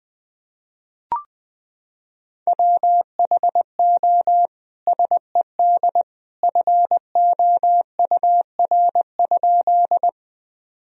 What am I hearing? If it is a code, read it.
Morse code, heard as a keyed tone, about 20 wpm, 700 Hz. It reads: WHO SED FOUR?
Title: Sentences from Top 300 Words and Abbreviations 20wpm